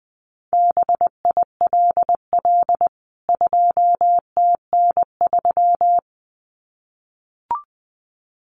Morse code, heard as a keyed tone, about 20 wpm, 700 Hz. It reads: BILL 2TN3
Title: CWT Exchange 20wpm